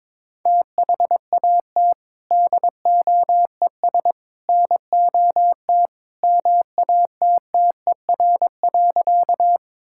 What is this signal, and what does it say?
Morse code, heard as a keyed tone, about 23 wpm, 700 Hz. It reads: THAT DOES NOT MATTER.
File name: practice / 22wpm / Sentences with 3-Word Phrases - 22wpm